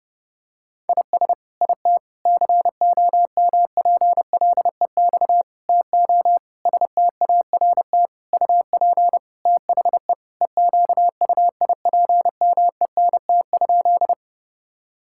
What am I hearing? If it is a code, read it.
Morse code, heard as a keyed tone, about 32 wpm, 700 Hz. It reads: IS IT COMPLEX TO START UP THE EQUIPMENT?
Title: Sentences from Top 1000 Words - Review Part 6 - 30wpm